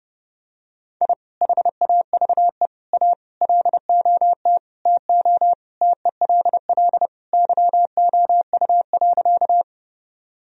Morse code, heard as a keyed tone, about 32 wpm, 700 Hz. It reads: I HAVE A LOT TO TELL YOU.
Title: Sentences from Top 1000 Words - Review Part 3 - 30wpm